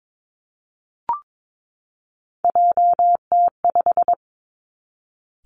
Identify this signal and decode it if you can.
Morse code, heard as a keyed tone, about 22 wpm, 700 Hz. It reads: JT5
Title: Three Challenging Letter-Number Combinations 22wpm